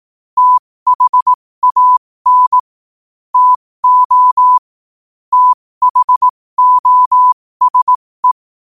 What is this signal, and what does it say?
Morse code, heard as a keyed tone, about 18 wpm, spaced as for 15 wpm, 1000 Hz. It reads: THAN TO THOSE
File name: qotdfiles / 15wpm